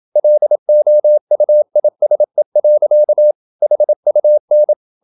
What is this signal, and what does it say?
Morse code, heard as a keyed tone, about 27 wpm, 600 Hz. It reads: LOUISE. HUN
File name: Ferie QRQ 27wpm